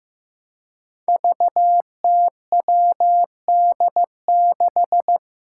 Morse code, heard as a keyed tone, about 15 wpm, 700 Hz. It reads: VTWD6
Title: Five Challenging Letter-Number Combinations 15wpm